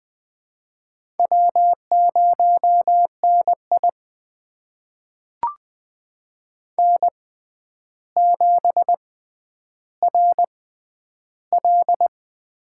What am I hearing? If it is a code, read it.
Morse code, heard as a keyed tone, about 20 wpm, spaced as for 10 wpm, 700 Hz. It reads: W0NI N7RL
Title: USA Call Signs - 4 Characters - 6x Character Spacing - 20wpm